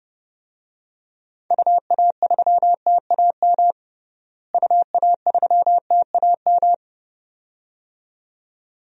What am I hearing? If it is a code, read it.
Morse code, heard as a keyed tone, about 30 wpm, 700 Hz. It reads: UA3TAM UA3TAM
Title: DXCC-100 Call Signs - 6 Characters - Repeated 2x - 30wpm